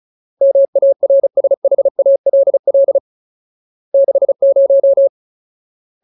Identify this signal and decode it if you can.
Morse code, heard as a keyed tone, about 37 wpm, 550 Hz. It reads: MARSHALL 60
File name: Ultimate Word List / 35WPM Ultimate